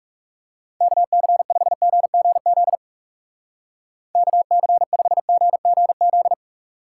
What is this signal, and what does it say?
Morse code, heard as a keyed tone, about 38 wpm, 700 Hz. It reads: KC5GGZ KC5GGZ